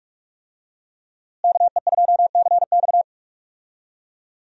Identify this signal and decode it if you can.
Morse code, heard as a keyed tone, about 46 wpm, 700 Hz. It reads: KE2CX
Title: USA Call Signs - 5 Characters - Repeated 2x - 45wpm